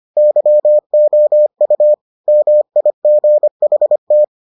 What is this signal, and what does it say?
Morse code, heard as a keyed tone, about 25 wpm, 600 Hz. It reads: YOU MIGHT